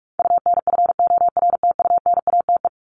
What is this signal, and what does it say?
Morse code, heard as a keyed tone, about 44 wpm, 700 Hz. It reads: UNFORTUNATE
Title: LY Suffix - Top 100 Words with Base Words 45wpm